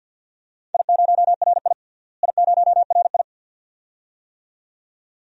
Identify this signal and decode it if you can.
Morse code, heard as a keyed tone, about 51 wpm, 700 Hz. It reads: I0RI I0RI
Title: DXCC-100 Call Signs - 4 Characters - Repeated 2x - 50wpm